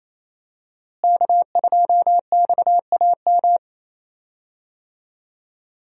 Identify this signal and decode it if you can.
Morse code, heard as a keyed tone, about 28 wpm, 700 Hz. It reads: K2XAM
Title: USA Call Signs - 5 Characters - Repeated 3x - 28wpm